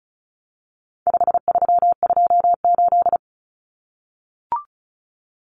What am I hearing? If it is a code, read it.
Morse code, heard as a keyed tone, about 34 wpm, 700 Hz. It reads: 5328
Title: Numbers - 4-Digit 35wpm